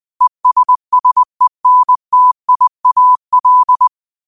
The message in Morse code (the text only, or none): ESSENTIAL